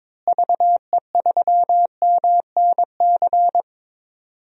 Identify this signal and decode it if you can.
Morse code, heard as a keyed tone, about 22 wpm, 700 Hz. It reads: VE3MNC
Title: DXCC-100 Call Signs - 6 Characters - Repeated 3x - 22wpm